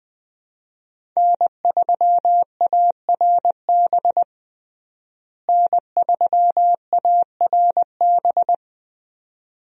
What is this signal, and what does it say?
Morse code, heard as a keyed tone, about 21 wpm, 700 Hz. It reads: N3ARB N3ARB
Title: USA Call Signs - 5 Characters - Repeated 2x - 20wpm